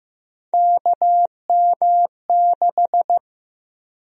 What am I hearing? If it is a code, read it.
Morse code, heard as a keyed tone, about 15 wpm, 700 Hz. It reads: KM6
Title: USA Call Sign Prefixes - 15wpm